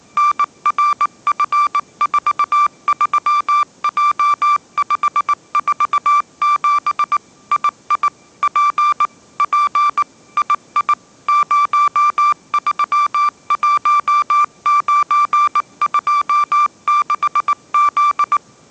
Morse code, heard as a keyed tone, about 19 wpm, 1180 Hz. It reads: NRF43J547 II P P II 031926Z